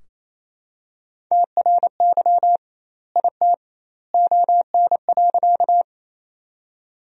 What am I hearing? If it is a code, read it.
Morse code, heard as a keyed tone, about 28 wpm, 700 Hz. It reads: TRY IT ON.
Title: 2x Word-Spacing - Sentences from Top 200 Words - 28wpm